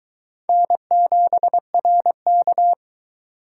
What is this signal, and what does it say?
Morse code, heard as a keyed tone, about 23 wpm, 700 Hz. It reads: N7RK